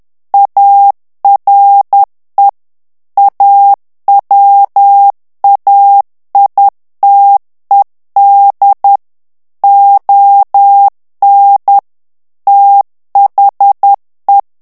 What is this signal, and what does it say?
Morse code, heard as a keyed tone, about 10 wpm, 800 Hz. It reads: ARE AWAITED ON THE
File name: morse mp3 / ANGLAIS texte clair 2 vitesse 10 mots minute